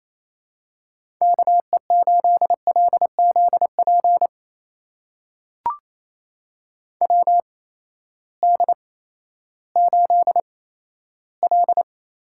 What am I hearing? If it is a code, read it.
Morse code, heard as a keyed tone, about 28 wpm, spaced as for 14 wpm, 700 Hz. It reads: KE8LZP WD8L
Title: USA Call Signs - 6 Characters - 8x Character Spacing - 28wpm